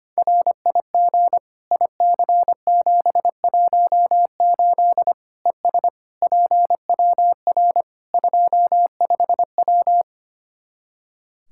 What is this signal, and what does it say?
Morse code, heard as a keyed tone, about 25 wpm, 700 Hz. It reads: RIG IC718 ES PWR 25W
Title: K1USN SST QSO Practice 25wpm